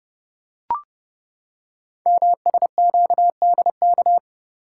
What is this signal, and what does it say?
Morse code, heard as a keyed tone, about 28 wpm, 700 Hz. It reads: MSQDK
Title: Code Groups - 5 Letters - 30wpm